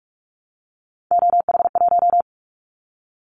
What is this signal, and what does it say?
Morse code, heard as a keyed tone, about 46 wpm, 700 Hz. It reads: OH1
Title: Callsign Prefixes for Top 50 DXCC - 45wpm